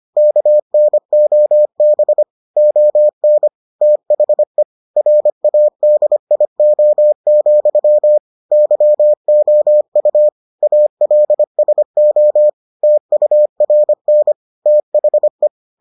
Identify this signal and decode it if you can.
Morse code, heard as a keyed tone, about 25 wpm, 600 Hz. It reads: KNOB ON THE RADIO, YOU ALSO TURN THE